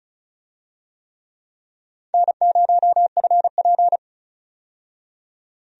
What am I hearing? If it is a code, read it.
Morse code, heard as a keyed tone, about 36 wpm, 700 Hz. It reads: N0FP
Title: USA Call Signs - 4 Characters - 35wpm